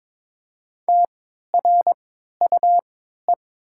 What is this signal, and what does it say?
Morse code, heard as a keyed tone, about 22 wpm, spaced as for 13 wpm, 700 Hz. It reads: TRUE
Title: 3x Character Spacing - Top 500 Words - 22wpm